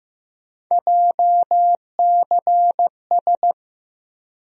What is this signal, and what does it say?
Morse code, heard as a keyed tone, about 15 wpm, 700 Hz. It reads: JCS
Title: USA Call Sign Postfixes - 3 Letters 15wpm